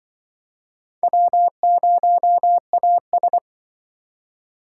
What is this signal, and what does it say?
Morse code, heard as a keyed tone, about 25 wpm, 700 Hz. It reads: W0AS
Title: Speed Racing - USA 1x2 Call Signs 20wpm